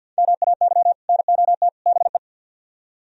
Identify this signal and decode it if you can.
Morse code, heard as a keyed tone, about 66 wpm, spaced as for 58 wpm, 700 Hz. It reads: MAY NOT BE
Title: Speed-Racing - Sets of 3 Words 50wpm